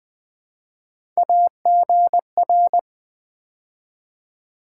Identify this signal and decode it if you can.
Morse code, heard as a keyed tone, about 20 wpm, 700 Hz. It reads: AGR
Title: Build A Word - 3 Letters At A Time - 9-Letter Words - 20wpm